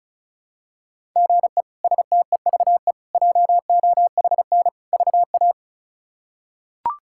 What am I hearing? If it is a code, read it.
Morse code, heard as a keyed tone, about 36 wpm, 700 Hz. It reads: GE STEVE JOHN VA
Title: K1USN SST QSO Practice - 35wpm